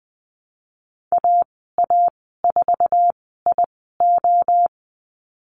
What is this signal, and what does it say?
Morse code, heard as a keyed tone, about 20 wpm, spaced as for 15 wpm, 700 Hz. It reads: AA4IO
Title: 069 Lesson 12g - Letter d - Call Signs 20wpm